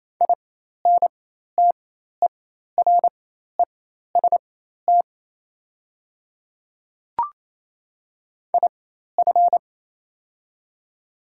Morse code, heard as a keyed tone, about 28 wpm, spaced as for 13 wpm, 700 Hz. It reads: INTEREST IF
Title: 4x Character Spacing - Top 500 Words - 28wpm